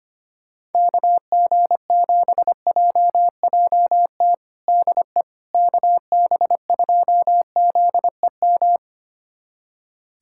Morse code, heard as a keyed tone, about 25 wpm, 700 Hz. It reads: KG7JJT DE KB2ZEM
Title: USA Callsign DE Callsign - 6 Characters - 25wpm